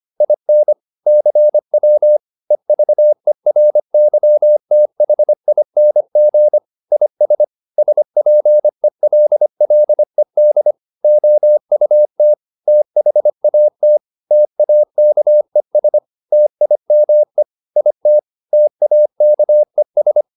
Morse code, heard as a keyed tone, about 25 wpm, 600 Hz. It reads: IN CW EVERYTHING IS SPELLED OUT THAT TAKES TIME IT TAKES
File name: SS319 25